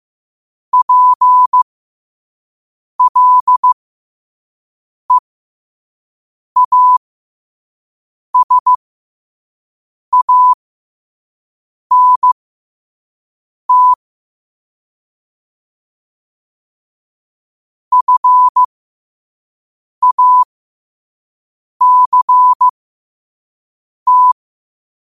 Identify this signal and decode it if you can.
Morse code, heard as a keyed tone, about 15 wpm, spaced as for 5 wpm, 1000 Hz. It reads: PLEASANT FACT
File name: qotdfiles / 5wpm